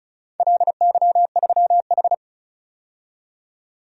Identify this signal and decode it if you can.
Morse code, heard as a keyed tone, about 36 wpm, 700 Hz. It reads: LY3H
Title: DXCC-100 Call Signs - 4 Characters - 35wpm